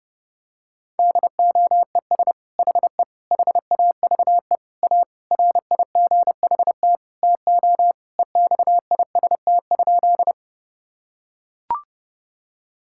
Morse code, heard as a keyed tone, about 30 wpm, 700 Hz. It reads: DOES HE HAVE A RIGHT TO EXIST?
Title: Sentences from Top 1000 Words - Review Part 5 - 30wpm